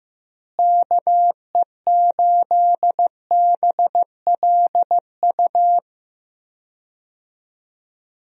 Morse code, heard as a keyed tone, about 16 wpm, 700 Hz. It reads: KE8BLU
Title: USA Call Signs - 6 Characters - Repeated 3x - 15wpm